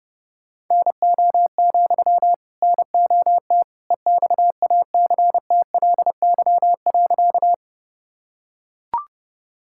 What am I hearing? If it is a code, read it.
Morse code, heard as a keyed tone, about 32 wpm, 700 Hz. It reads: NO, NOT EXACTLY.
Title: Sentences from Top 1000 Words - Review Part 6 - 30wpm